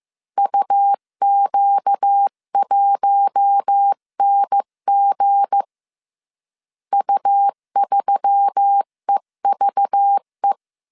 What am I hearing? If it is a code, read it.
Morse code, heard as a keyed tone, about 15 wpm, 800 Hz. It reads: UQ1NG U3EVE
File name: CW / DET / 1 9 1500C S B 14wpm 3302